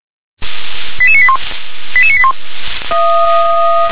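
Five-tone selective call (ZVEI), heard as quick runs of five tones, tones of about 70 ms. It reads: 88981 88981
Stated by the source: ZVEI five-tone call received on an SDR.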